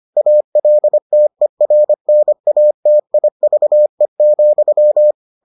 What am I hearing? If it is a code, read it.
Morse code, heard as a keyed tone, about 25 wpm, 600 Hz. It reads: ALTERNATIVE,